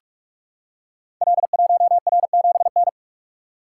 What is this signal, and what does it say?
Morse code, heard as a keyed tone, about 44 wpm, 700 Hz. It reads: L1R7N
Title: Code Groups - 5 Letter:Numbers - 45wpm